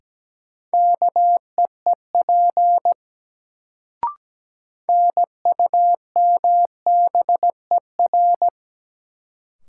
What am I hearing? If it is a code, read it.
Morse code, heard as a keyed tone, about 17 wpm, 700 Hz. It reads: KEEP NUMBER
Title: Sets of 2 Words - New Word Drill - 17wpm